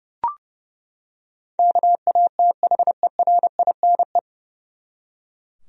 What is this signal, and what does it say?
Morse code, heard as a keyed tone, about 29 wpm, 700 Hz. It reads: KATHERINE
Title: USA Common Names - 30wpm